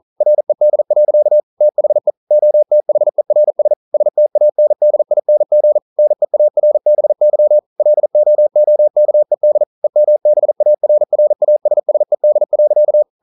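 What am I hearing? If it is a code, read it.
Morse code, heard as a keyed tone, about 40 wpm, 600 Hz. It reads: RED. THE OTHERS STANDING NEARBY LOOKED EMBARRASSED.
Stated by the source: Morse code practice at 41 wpm, from CW med Gnister.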